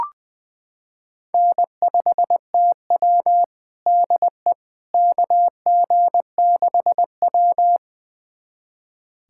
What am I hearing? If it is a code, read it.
Morse code, heard as a keyed tone, about 20 wpm, 700 Hz. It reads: N5TW DE KG6W
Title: USA Callsign DE Callsign - 4 Characters - 20wpm